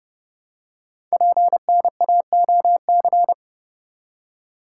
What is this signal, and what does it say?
Morse code, heard as a keyed tone, about 30 wpm, 700 Hz. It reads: PNAOC
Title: Code Groups - 5 Letters - 30wpm